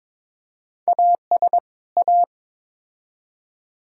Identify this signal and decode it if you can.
Morse code, heard as a keyed tone, about 22 wpm, 700 Hz. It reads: AS A
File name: practice / 22wpm / Build A Phrase - 5-Word Phrases - 22wpm